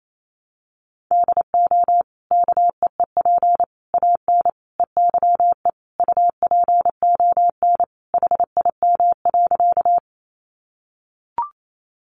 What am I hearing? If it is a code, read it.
Morse code, heard as a keyed tone, about 28 wpm, 700 Hz. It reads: DO KEEP AN EYE UPON HIM.
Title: Sentences from Top 1100 Words 28wpm